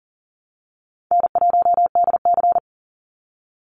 Morse code, heard as a keyed tone, about 40 wpm, 700 Hz. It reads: N1DC
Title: Easy Call Signs - 40wpm